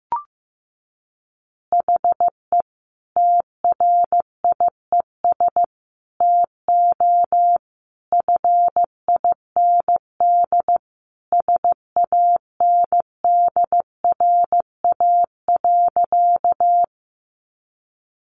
Morse code, heard as a keyed tone, about 15 wpm, 700 Hz. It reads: HE TRIES TO FIND SANDRA.
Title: Easy Phrases 15wpm